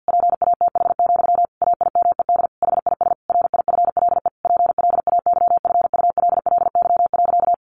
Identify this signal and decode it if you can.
Morse code, heard as a keyed tone, about 50 wpm, 700 Hz. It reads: PATH, AIMED HIS RIFLE PLAYFULLY.
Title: A Sound Of Thunder 50wpm